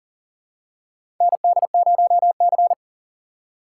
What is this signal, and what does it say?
Morse code, heard as a keyed tone, about 40 wpm, 700 Hz. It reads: ND0C
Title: USA Call Signs - 4 Characters - 4x Character Spacing - 40wpm